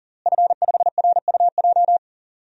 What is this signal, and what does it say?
Morse code, heard as a keyed tone, about 37 wpm, 700 Hz. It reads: F5RUJ
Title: DXCC-100 Call Signs - 5 Characters - Repeated 3x - 40wpm